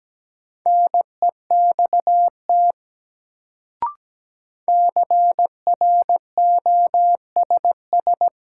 Morse code, heard as a keyed tone, about 17 wpm, 700 Hz. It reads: NEXT CROSS
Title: Sentences from Top 300 Words - New Word Drill - 17wpm